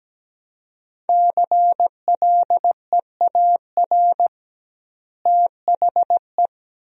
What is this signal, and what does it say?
Morse code, heard as a keyed tone, about 17 wpm, 700 Hz. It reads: CLEAR THE
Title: 2x Word-Spacing - Sentences from Top 500 Words - 17wpm